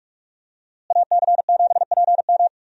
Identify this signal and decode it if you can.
Morse code, heard as a keyed tone, about 46 wpm, 700 Hz. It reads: AC7PM
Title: USA Call Signs - 5 Characters - Repeated 2x - 45wpm